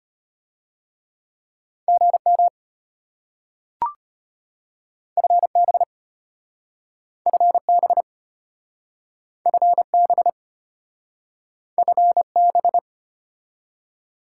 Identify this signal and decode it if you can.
Morse code, heard as a keyed tone, about 34 wpm, 690 Hz. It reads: GM FB FB FB FB